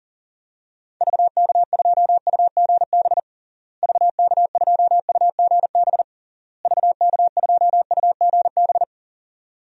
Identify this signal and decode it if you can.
Morse code, heard as a keyed tone, about 37 wpm, 700 Hz. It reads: VK2UGB VK2UGB VK2UGB
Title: DXCC-100 Call Signs - 6 Characters - Repeated 3x - 40wpm